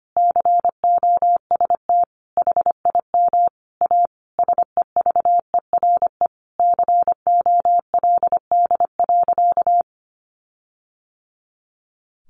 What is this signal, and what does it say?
Morse code, heard as a keyed tone, about 25 wpm, 700 Hz. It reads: COST HIM A SEVERE COLD.
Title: Sentences from Top 1400 Words 25wpm